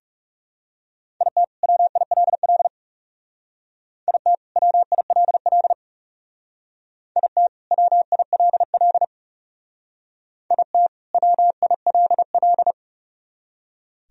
Morse code, heard as a keyed tone, about 36 wpm, 700 Hz. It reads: IT WILL IT WILL IT WILL IT WILL
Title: Speed-Racing - Sets of 2 Words 30wpm